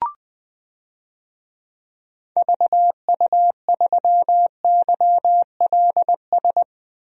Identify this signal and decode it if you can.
Morse code, heard as a keyed tone, about 21 wpm, 700 Hz. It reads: VU3YLS
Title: DXCC-100 Call Signs - 6 Characters - 20wpm